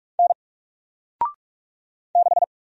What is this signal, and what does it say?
Morse code, heard as a keyed tone, about 46 wpm, 700 Hz. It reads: N 6
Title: Single Character (Letter:Number:Punctuation:Prosign) Alt 45wpm